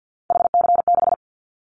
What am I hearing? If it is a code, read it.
Morse code, heard as a keyed tone, about 50 wpm, 690 Hz. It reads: HC6
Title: Callsign Prefixes for Top 100 DXCC - 50wpm